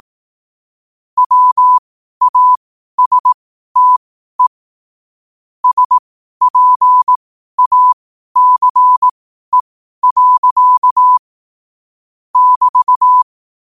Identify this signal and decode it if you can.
Morse code, heard as a keyed tone, about 18 wpm, spaced as for 12 wpm, 1000 Hz. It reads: WASTE SPACE. <BT>
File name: qotdfiles / 12wpm